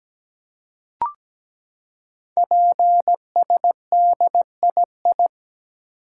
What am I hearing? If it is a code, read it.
Morse code, heard as a keyed tone, about 17 wpm, 700 Hz. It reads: PSDII
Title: Code Groups - 5 Letter:Numbers - 17wpm